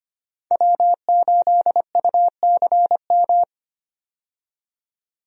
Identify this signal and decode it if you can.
Morse code, heard as a keyed tone, about 25 wpm, 700 Hz. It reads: W8UCM
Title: USA Call Signs - 5 Characters - 25wpm